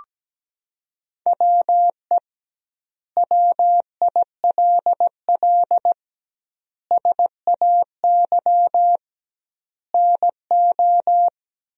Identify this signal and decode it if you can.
Morse code, heard as a keyed tone, about 17 wpm, 700 Hz. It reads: WE WILL SAY NO
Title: 2x Word-Spacing - Sentences from Top 200 Words - 17wpm